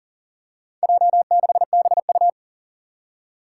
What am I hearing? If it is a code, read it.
Morse code, heard as a keyed tone, about 40 wpm, 700 Hz. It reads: J6BU
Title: Four Challenging Letter-Number Combinations 40wpm